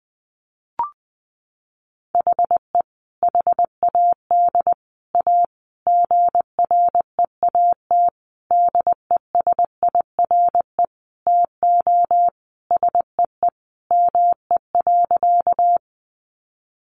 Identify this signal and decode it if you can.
Morse code, heard as a keyed tone, about 20 wpm, 700 Hz. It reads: HE HAD A GREAT DESIRE TO SEE ME.
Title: Sentences from Top 1100 Words 20wpm